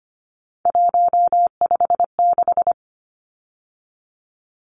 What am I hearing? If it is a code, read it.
Morse code, heard as a keyed tone, about 25 wpm, 700 Hz. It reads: 156
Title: Numbers - 3-Digit 25wpm